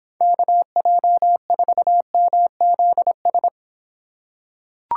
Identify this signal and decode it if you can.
Morse code, heard as a keyed tone, about 24 wpm, 700 Hz. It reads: KJ4MZS